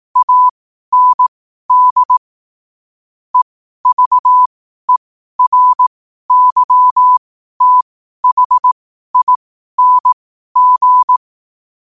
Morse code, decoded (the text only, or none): AND EVERYTHING